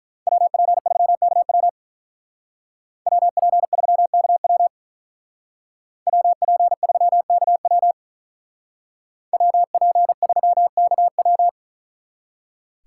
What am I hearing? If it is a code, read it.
Morse code, heard as a keyed tone, about 43 wpm, 700 Hz. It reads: WP3KW WP3KW WP3KW WP3KW
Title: Speed Racing - USA 2x2 Call Signs 35wpm